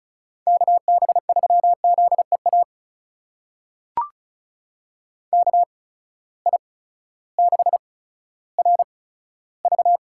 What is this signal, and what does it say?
Morse code, heard as a keyed tone, about 36 wpm, spaced as for 16 wpm, 700 Hz. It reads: KB3ZEA KI6RV
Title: USA Call Signs - 6 Characters - 8x Character Spacing - 35wpm